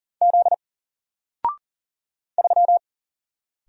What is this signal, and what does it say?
Morse code, heard as a keyed tone, about 39 wpm, 700 Hz. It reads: Z 3
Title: Single Letter-Number 40wpm